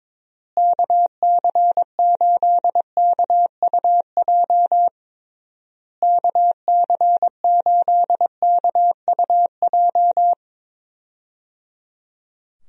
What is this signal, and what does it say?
Morse code, heard as a keyed tone, about 22 wpm, 700 Hz. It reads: KC8KUJ KC8KUJ
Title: USA Call Signs - 6 Characters - Repeated 3x - 22wpm